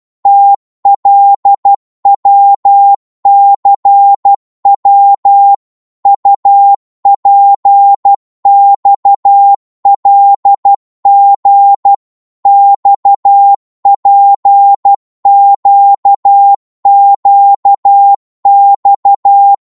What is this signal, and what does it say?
Morse code, heard as a keyed tone, about 12 wpm, 800 Hz. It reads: TLWCW UPXLG XPQQX
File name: Morse / ltr-12wpm